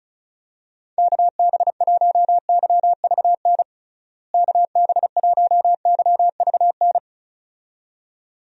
Morse code, heard as a keyed tone, about 36 wpm, 700 Hz. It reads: KB1YVN KB1YVN
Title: USA Call Signs - 6 Characters - Repeated 2x - 35wpm